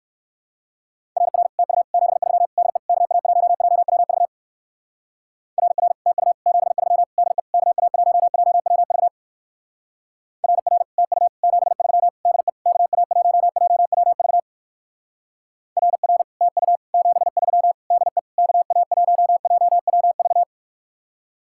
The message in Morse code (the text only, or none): RR TU 73 DE KA1JWV RR TU 73 DE KA1JWV RR TU 73 DE KA1JWV RR TU 73 DE KA1JWV